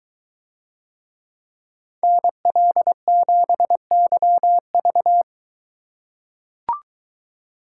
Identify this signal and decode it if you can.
Morse code, heard as a keyed tone, about 21 wpm, 700 Hz. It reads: NL7YV